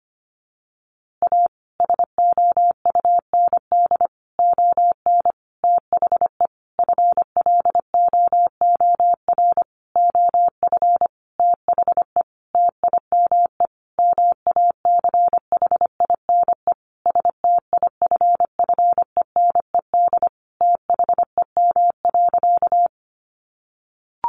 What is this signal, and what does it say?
Morse code, heard as a keyed tone, about 25 wpm, 700 Hz. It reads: A SOUND ON THE FLOOR OF THE TIME MACHINE STIFFENED THEM.
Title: A Sound Of Thunder 25wpm